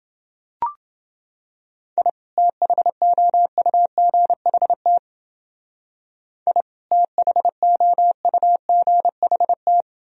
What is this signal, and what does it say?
Morse code, heard as a keyed tone, about 29 wpm, 700 Hz. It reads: I THOUGHT I THOUGHT